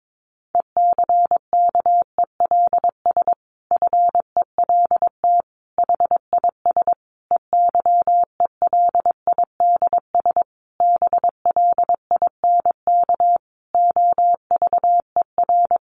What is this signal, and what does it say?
Morse code, heard as a keyed tone, about 22 wpm, 700 Hz. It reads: ECKELS FELT HIS EYELIDS BLINK OVER
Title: A Sound Of Thunder 22wpm